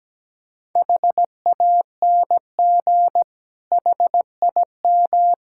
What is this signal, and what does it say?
Morse code, heard as a keyed tone, about 17 wpm, 700 Hz. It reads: HANG HIM
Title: Sentences from Top 1600 Words 17wpm